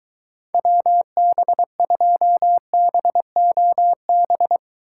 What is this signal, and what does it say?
Morse code, heard as a keyed tone, about 23 wpm, 700 Hz. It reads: WB2BOB